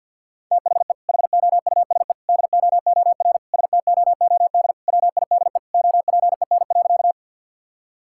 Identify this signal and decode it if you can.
Morse code, heard as a keyed tone, about 51 wpm, 700 Hz. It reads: THE HOUSE DOOR STOOD WIDE OPEN.
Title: Sentences from Top 1000 Words - Review Part 6 - 50wpm